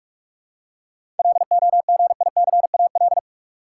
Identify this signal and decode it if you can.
Morse code, heard as a keyed tone, about 46 wpm, 700 Hz. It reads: LOGICAL
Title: Seven Letter Words - 45wpm